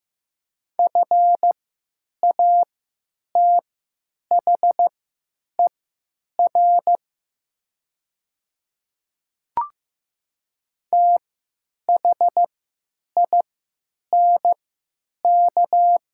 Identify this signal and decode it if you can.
Morse code, heard as a keyed tone, about 15 wpm, spaced as for 9 wpm, 700 Hz. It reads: FATHER THINK
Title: 3x Character Spacing - Top 300 Words - 15wpm